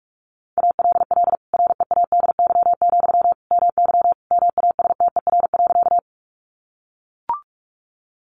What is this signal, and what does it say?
Morse code, heard as a keyed tone, about 44 wpm, 700 Hz. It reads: ALL READY, MY MASTER.
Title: Sentences from Top 1300 Words 45wpm